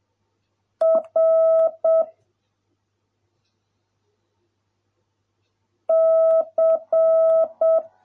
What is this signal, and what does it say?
Morse code, heard as a keyed tone, about 7 wpm, 650 Hz. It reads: R C